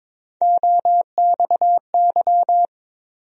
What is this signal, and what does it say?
Morse code, heard as a keyed tone, about 22 wpm, 700 Hz. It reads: OXY
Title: USA Call Sign Postfixes - 3 Letters 22wpm